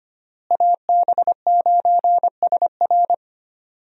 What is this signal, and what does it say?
Morse code, heard as a keyed tone, about 25 wpm, 700 Hz. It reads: AB9SR
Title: USA Call Signs - 5 Characters - Repeated 2x - 25wpm